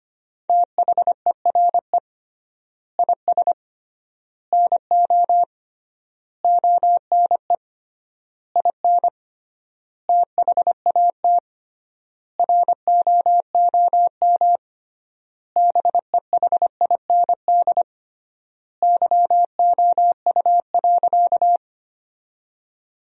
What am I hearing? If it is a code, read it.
Morse code, heard as a keyed tone, about 25 wpm, 700 Hz. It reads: THERE IS NO ONE IN THAT ROOM BEHIND YOU.
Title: 3x Word-Spacing - Sentences from Top 500 Words - 25wpm